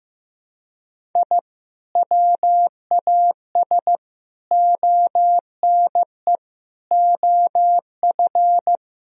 Morse code, heard as a keyed tone, about 15 wpm, 700 Hz. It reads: I WAS ONE OF
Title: Sentences from Top 1000 Words - Review Part 1 - 15wpm